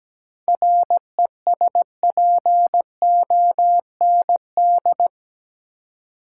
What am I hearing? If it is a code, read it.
Morse code, heard as a keyed tone, about 17 wpm, 700 Hz. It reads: RESPOND
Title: Seven Letter Words - 17wpm